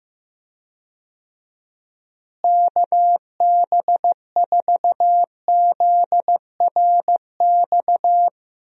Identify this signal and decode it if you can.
Morse code, heard as a keyed tone, about 15 wpm, 700 Hz. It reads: KB4ZRX
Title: USA Call Signs - 6 Characters - Repeated 2x - 15wpm